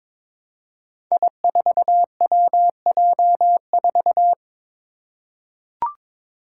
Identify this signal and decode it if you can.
Morse code, heard as a keyed tone, about 22 wpm, 700 Hz. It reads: I4WJ4
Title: Five Challenging Letter-Number Combinations 22wpm